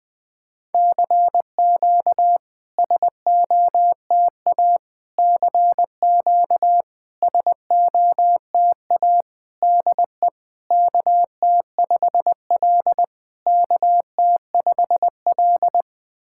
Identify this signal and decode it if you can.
Morse code, heard as a keyed tone, about 21 wpm, 690 Hz. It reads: CQ SOTA CQ SOTA DE KT5L KT5L
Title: SOTA QSO Practice 20wpm